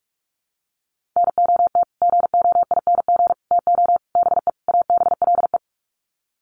Morse code, heard as a keyed tone, about 44 wpm, 700 Hz. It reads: NOT GOING TO BE ABLE
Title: Sets of 5 Words 45wpm